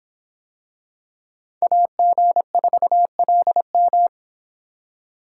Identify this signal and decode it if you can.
Morse code, heard as a keyed tone, about 26 wpm, 700 Hz. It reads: AG4LM